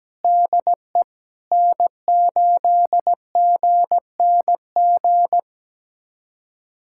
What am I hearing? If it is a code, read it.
Morse code, heard as a keyed tone, about 17 wpm, 700 Hz. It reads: DE N8GNG
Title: USA Callsign DE Callsign 17wpm